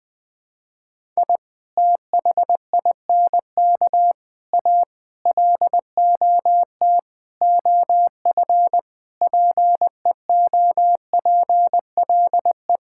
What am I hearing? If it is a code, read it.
Morse code, heard as a keyed tone, about 20 wpm, 700 Hz. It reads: I THINK A LOT OF PEOPLE
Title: Sentences with 5-Word Phrases - 20wpm